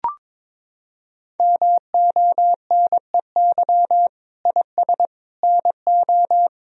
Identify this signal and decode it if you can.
Morse code, heard as a keyed tone, about 22 wpm, 700 Hz. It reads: MONEY IS NO
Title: Sentences from Top 1000 Words - Review Part 3 - 22wpm